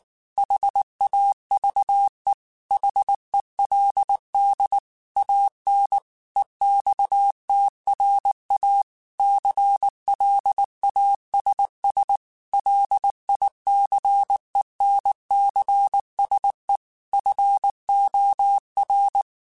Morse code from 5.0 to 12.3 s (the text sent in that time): AN EXTRA CLASS